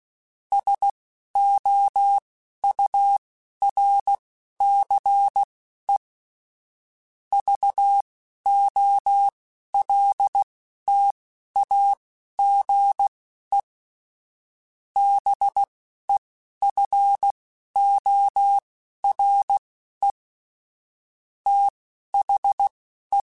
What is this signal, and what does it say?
Morse code, heard as a keyed tone, about 16 wpm, spaced as for 11 wpm, 800 Hz. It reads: SOURCE VOLTAGE BEFORE THE
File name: Technical Subjects / Technical Subjects-mp3s / XC @10